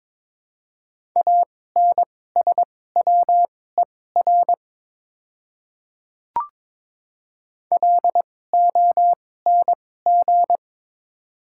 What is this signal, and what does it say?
Morse code, heard as a keyed tone, about 22 wpm, spaced as for 16 wpm, 700 Hz. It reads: ANSWER LONG
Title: 2x Character Spacing - Top 300 Words - 22wpm